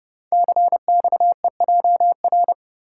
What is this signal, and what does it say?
Morse code, heard as a keyed tone, about 30 wpm, 700 Hz. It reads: CXEJR
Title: Code Groups - 5 Letter:Numbers - 30wpm